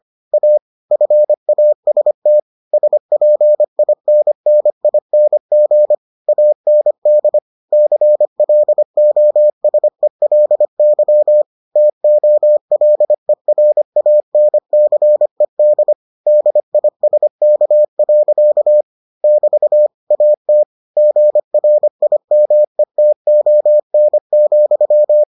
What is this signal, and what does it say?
Morse code, heard as a keyed tone, about 25 wpm, 600 Hz. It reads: A FAST SPINNING AND CLOSELY TOLERANCED DISK. <BT> AT GRIMETON,